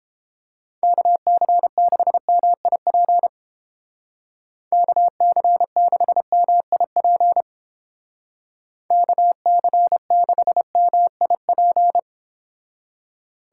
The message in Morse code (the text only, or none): KC6MIP KC6MIP KC6MIP